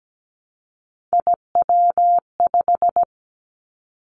Morse code, heard as a keyed tone, about 17 wpm, 700 Hz. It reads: IW5
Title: Three Challenging Letter-Number Combinations 17wpm